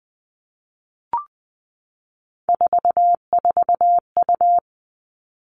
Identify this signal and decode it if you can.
Morse code, heard as a keyed tone, about 20 wpm, 700 Hz. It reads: 44U
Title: Three Challenging Letter-Number Combinations 20wpm